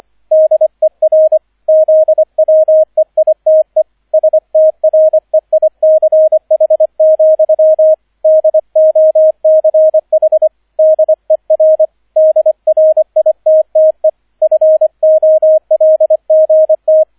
Morse code, heard as a keyed tone, about 25 wpm, 630 Hz, with radio noise. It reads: DER ZWEITE STREICH, DOCH DER DRITTE FOLGT